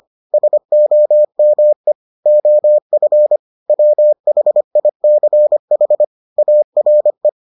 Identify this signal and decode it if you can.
Morse code, heard as a keyed tone, about 25 wpm, 600 Hz. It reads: SOME OF WHICH ARE